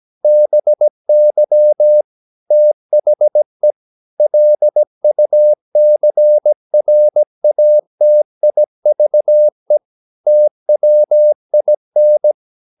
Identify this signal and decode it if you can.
Morse code, heard as a keyed tone, about 17 wpm, 600 Hz. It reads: BY THE LUCRATIVE TWIN
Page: Romanen Never af Ken Follet på engelsk.